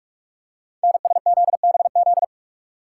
Morse code, heard as a keyed tone, about 44 wpm, 700 Hz. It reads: NSZBZ
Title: Code Groups - 5 Letters - 45wpm